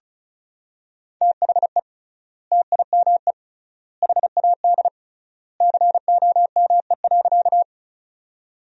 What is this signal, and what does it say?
Morse code, heard as a keyed tone, about 36 wpm, 700 Hz. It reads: THE TIME HAD COME.
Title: 3x Word-Spacing - Sentences from Top 100 Words - 35wpm